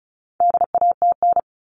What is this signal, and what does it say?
Morse code, heard as a keyed tone, about 34 wpm, 700 Hz. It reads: DATN
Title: All Cut Numbers - 4-Digit 35wpm